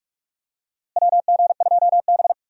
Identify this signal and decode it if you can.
Morse code, heard as a keyed tone, about 46 wpm, 700 Hz. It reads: WG2B
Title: USA Call Signs - 4 Characters - Repeated 2x - 45wpm